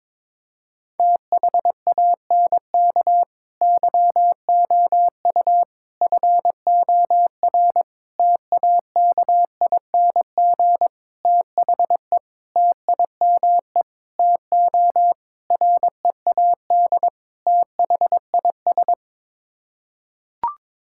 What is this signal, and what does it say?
Morse code, heard as a keyed tone, about 22 wpm, 700 Hz. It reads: THANK YOU FOR TAKING THE TIME TO READ THIS
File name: Sets of 9 Words - 22wpm